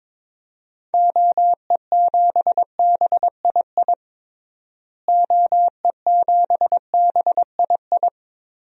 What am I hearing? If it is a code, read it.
Morse code, heard as a keyed tone, about 22 wpm, 700 Hz. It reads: OE7BII OE7BII
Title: DXCC-100 Call Signs - 6 Characters - Repeated 3x - 22wpm